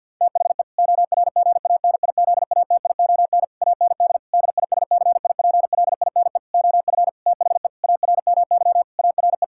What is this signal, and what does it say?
Morse code, heard as a keyed tone, about 48 wpm, 700 Hz. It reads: THE ORGANIZATION AND DISCIPLINE OF THE ARMY ARE
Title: Sentences from Top 1000 Words - Review Part 6 - 50wpm